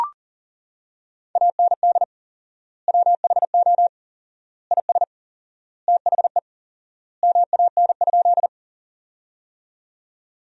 Morse code, heard as a keyed tone, about 40 wpm, 700 Hz. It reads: AND WHO IS THE MAN?
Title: 4x Word-Spacing - Sentences from Top 200 Words - 40wpm